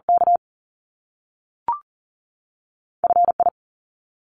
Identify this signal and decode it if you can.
Morse code, heard as a keyed tone, about 40 wpm, 700 Hz. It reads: K F I
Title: Top Two Letter Combinations in Words 40wpm